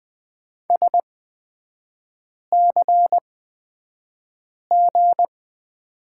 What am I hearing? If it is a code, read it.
Morse code, heard as a keyed tone, about 20 wpm, spaced as for 5 wpm, 700 Hz. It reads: SCG
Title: 250 Lesson 40e - BK - Three Characters 20wpm